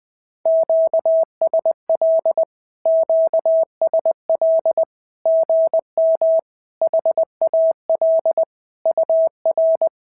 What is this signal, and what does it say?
Morse code, heard as a keyed tone, about 20 wpm, 650 Hz. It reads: QSL QSL GM HAL UR
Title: SOTA QSO Practice 20wpm